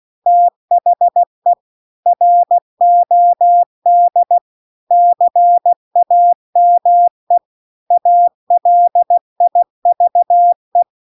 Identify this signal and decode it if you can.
Morse code, heard as a keyed tone, about 16 wpm, 700 Hz. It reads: THE ROD CAME ALIVE
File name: CW-The-Big-Twoheaded-River-2-16WPM